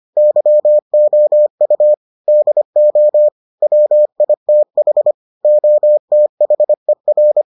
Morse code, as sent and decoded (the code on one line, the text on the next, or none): -.-- --- ..- / -.. --- / .-- .. - .... / --- - .... .-.
YOU DO WITH OTHER